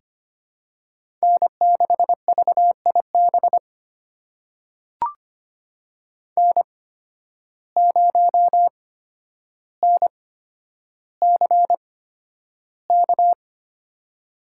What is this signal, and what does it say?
Morse code, heard as a keyed tone, about 25 wpm, spaced as for 11 wpm, 700 Hz. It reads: N6VIB N0NCK
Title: USA Call Signs - 5 Characters - 8x Character Spacing - 25wpm